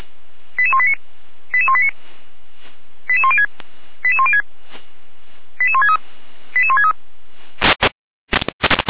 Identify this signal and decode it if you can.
Five-tone selective call (ZVEI), heard as quick runs of five tones, tones of about 80 ms. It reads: 88189 88189 88106 88106 88163 88163